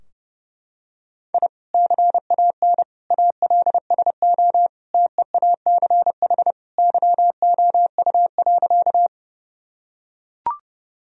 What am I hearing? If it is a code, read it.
Morse code, heard as a keyed tone, about 30 wpm, 700 Hz. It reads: I CAN ALSO TEACH YOU.
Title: Sentences from Top 1000 Words - Review Part 2 - 30wpm